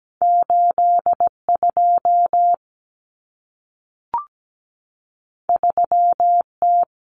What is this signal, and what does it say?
Morse code, heard as a keyed tone, about 17 wpm, 700 Hz. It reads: 82 3T
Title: Common Cut Numbers - 2-Digit 17wpm